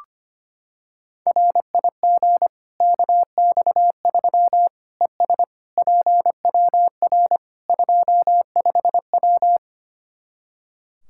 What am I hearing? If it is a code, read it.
Morse code, heard as a keyed tone, about 25 wpm, 700 Hz. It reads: RIG KX3 ES PWR 25W
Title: Practice QSO - Exchange 2 - Info - 25wpm